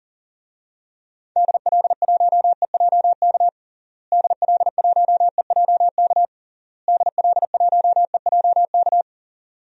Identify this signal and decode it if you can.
Morse code, heard as a keyed tone, about 40 wpm, 700 Hz. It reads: DL1EJK DL1EJK DL1EJK
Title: DXCC-100 Call Signs - 6 Characters - Repeated 3x - 40wpm